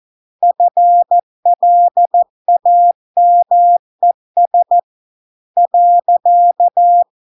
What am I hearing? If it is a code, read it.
Morse code, heard as a keyed tone, about 14 wpm, 700 Hz. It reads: FLAMES .